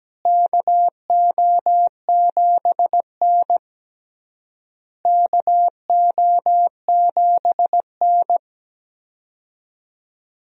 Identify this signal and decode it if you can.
Morse code, heard as a keyed tone, about 17 wpm, 700 Hz. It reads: KO7N KO7N
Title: USA Call Signs - 4 Characters - Repeated 3x - 17wpm